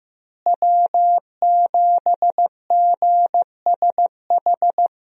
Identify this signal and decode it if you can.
Morse code, heard as a keyed tone, about 15 wpm, 700 Hz. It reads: W7GSH
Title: USA Call Signs - 5 Characters - Repeated 2x - 15wpm